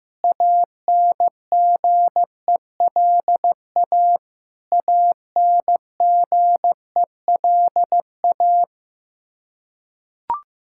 Morse code, heard as a keyed tone, about 15 wpm, 700 Hz. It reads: ANGELA ANGELA
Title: Practice QSO - Exchange 1 - Beginning - 15wpm